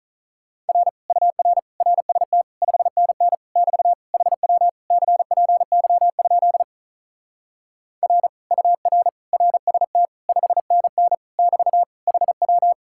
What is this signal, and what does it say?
Morse code, heard as a keyed tone, about 43 wpm, 700 Hz. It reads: R UR RST 5NN <BT> HW CPY? R UR RST 5NN <BT> HW
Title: Speed-Racing Common QSO Phrases - 35wpm